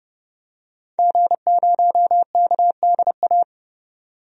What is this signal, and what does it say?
Morse code, heard as a keyed tone, about 30 wpm, 700 Hz. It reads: G0KDA
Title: DXCC-100 Call Signs - 5 Characters - 30wpm